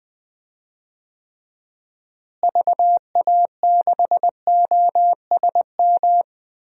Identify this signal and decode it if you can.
Morse code, heard as a keyed tone, about 19 wpm, 700 Hz. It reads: VA6OSM
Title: DXCC-100 Call Signs - 6 Characters - Repeated 2x - 20wpm